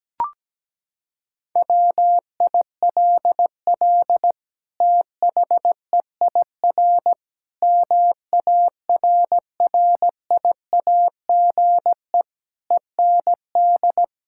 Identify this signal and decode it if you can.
Morse code, heard as a keyed tone, about 17 wpm, 700 Hz. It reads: WILL THEIR MARRIAGE END
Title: Sentences from Top 1000 Words - Review Part 6 - 17wpm